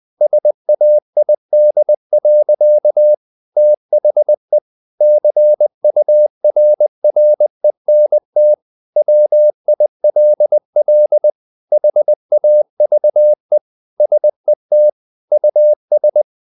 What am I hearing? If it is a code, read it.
Morse code, heard as a keyed tone, about 20 wpm, 600 Hz. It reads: SAID. THE CURRENT WILL HAVE SET US
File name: the old man and the sea / the old man and the sea part 1 20wpm